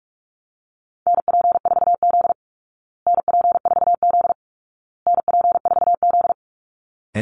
Morse code, heard as a keyed tone, about 44 wpm, 700 Hz. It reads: NP4Z NP4Z NP4Z
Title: Call Signs Repeated 2x - 45wpm